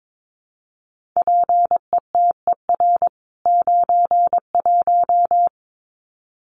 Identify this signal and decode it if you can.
Morse code, heard as a keyed tone, about 22 wpm, 700 Hz. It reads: PETER 91
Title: CWT Exchange 22wpm